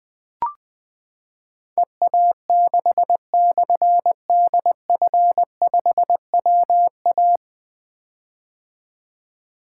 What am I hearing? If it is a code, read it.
Morse code, heard as a keyed tone, about 20 wpm, 700 Hz. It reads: EA6/DF5WA
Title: Difficult Call Signs 20wpm